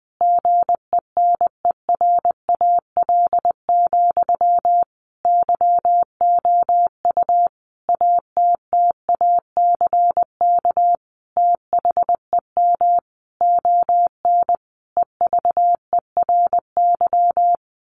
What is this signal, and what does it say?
Morse code, heard as a keyed tone, about 20 wpm, 700 Hz. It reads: GENERAL, YOU ATTACK THEM ON EVERY
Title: Sentences from Top 1500 Words 20wpm